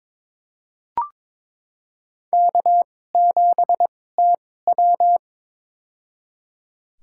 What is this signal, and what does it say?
Morse code, heard as a keyed tone, about 22 wpm, spaced as for 16 wpm, 700 Hz. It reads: K7TW
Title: USA Call Signs - 4 Characters - 2x Character Spacing - 22wpm